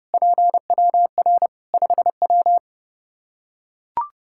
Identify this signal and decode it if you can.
Morse code, heard as a keyed tone, about 30 wpm, 700 Hz. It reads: PWR 5W
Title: Practice QSO - Exchange 2 - Info - 30wpm